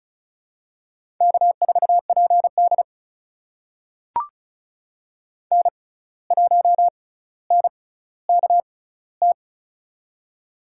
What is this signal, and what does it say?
Morse code, heard as a keyed tone, about 34 wpm, spaced as for 17 wpm, 700 Hz. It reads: K4PD N1NKT
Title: USA Call Signs - 6x Character Spacing - 35wpm